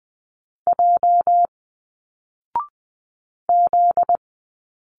Morse code, heard as a keyed tone, about 20 wpm, 700 Hz. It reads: J Z